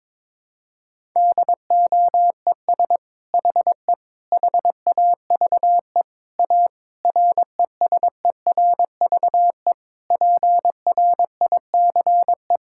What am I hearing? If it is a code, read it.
Morse code, heard as a keyed tone, about 23 wpm, 700 Hz. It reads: DOES HE HAVE A RESERVE PRICE
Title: Sentences from Top 1000 Words - Review Part 6 - 22wpm